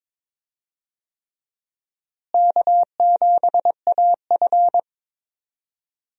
Morse code, heard as a keyed tone, about 22 wpm, 700 Hz. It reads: K7AF
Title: USA Call Signs - 4 Characters - Repeated 3x - 22wpm